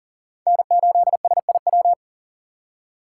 Morse code, heard as a keyed tone, about 40 wpm, 700 Hz. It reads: N8SIW
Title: USA Call Signs - 6x Character Spacing - 40wpm